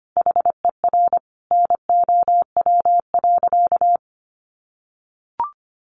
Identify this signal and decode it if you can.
Morse code, heard as a keyed tone, about 25 wpm, 700 Hz. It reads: HER NOW.
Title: Sentences from Top 1400 Words 25wpm